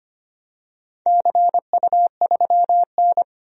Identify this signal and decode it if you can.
Morse code, heard as a keyed tone, about 25 wpm, 700 Hz. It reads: CU3N
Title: DXCC-100 Call Signs - 4 Characters - 25wpm